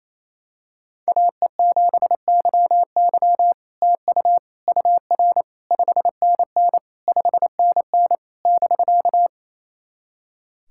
Morse code, heard as a keyed tone, about 27 wpm, 700 Hz. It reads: AE7YY TU UR 5NN 5NN <BK>
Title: SOTA QSO Practice 28wpm